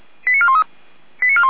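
Five-tone selective call (ZVEI), heard as quick runs of five tones, tones of about 70 ms. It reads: 89413
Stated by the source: ZVEI five-tone selective call received by SDR.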